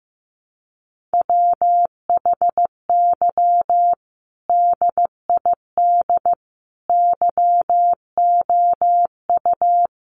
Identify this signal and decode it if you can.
Morse code, heard as a keyed tone, about 15 wpm, 700 Hz. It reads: WHY DID YOU
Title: Sentences from Top 1200 Words 15wpm